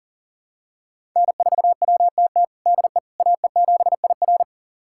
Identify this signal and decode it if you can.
Morse code, heard as a keyed tone, about 40 wpm, 700 Hz. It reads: N4WTT DE AE7IR
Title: USA Callsign DE Callsign - 5 Characters - 40wpm